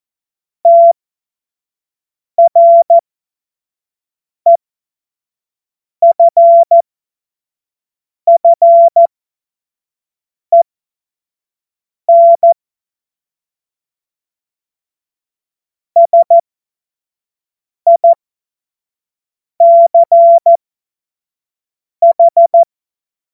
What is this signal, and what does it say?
Morse code, heard as a keyed tone, about 14 wpm, spaced as for 5 wpm, 680 Hz. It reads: TREFFEN SICH
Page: cw-witze